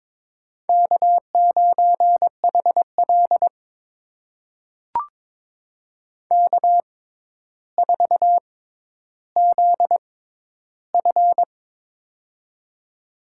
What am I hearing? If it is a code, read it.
Morse code, heard as a keyed tone, about 22 wpm, spaced as for 11 wpm, 700 Hz. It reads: K9HL K4ZF
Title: USA Call Signs - 4 Characters - 6x Character Spacing - 22wpm